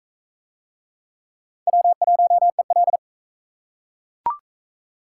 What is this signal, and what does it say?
Morse code, heard as a keyed tone, about 38 wpm, 700 Hz. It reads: W1EL